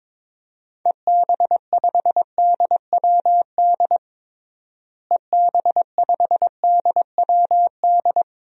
Morse code, heard as a keyed tone, about 21 wpm, 700 Hz. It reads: EB5DWD EB5DWD
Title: DXCC-100 Call Signs - 6 Characters - Repeated 3x - 22wpm